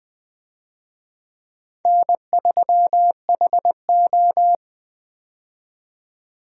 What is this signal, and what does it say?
Morse code, heard as a keyed tone, about 20 wpm, 700 Hz. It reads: N3HO
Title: USA Call Signs - 4 Characters - 20wpm